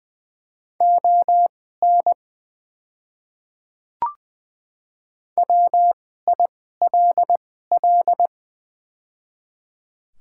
Morse code, heard as a keyed tone, about 20 wpm, spaced as for 14 wpm, 700 Hz. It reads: ON WILL
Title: 2x Character Spacing - Top 200 Words - 20wpm